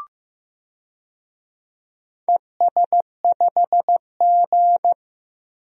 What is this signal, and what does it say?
Morse code, heard as a keyed tone, about 15 wpm, 700 Hz. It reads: ES5G
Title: DXCC-100 Call Signs - 4 Characters - 15wpm